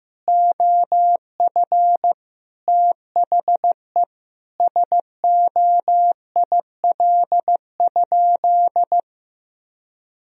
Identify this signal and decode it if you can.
Morse code, heard as a keyed tone, about 15 wpm, 700 Hz. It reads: OF THE SOIL?
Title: Sentences from Top 1000 Words - Review Part 4 - 15wpm